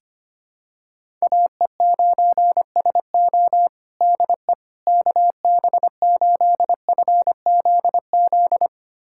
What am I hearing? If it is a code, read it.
Morse code, heard as a keyed tone, about 25 wpm, 700 Hz. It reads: AE9SO DE KB8FZZ
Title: USA Callsign DE Callsign 25wpm